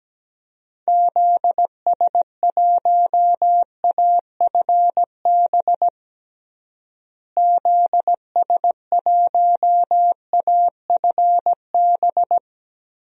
Morse code, heard as a keyed tone, about 17 wpm, 700 Hz. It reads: ZS1AFB ZS1AFB
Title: DXCC-100 Call Signs - 6 Characters - Repeated 3x - 17wpm